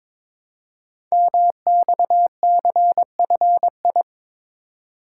Morse code, heard as a keyed tone, about 22 wpm, 700 Hz. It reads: MXCFI
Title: Code Groups - 5 Letters - 22wpm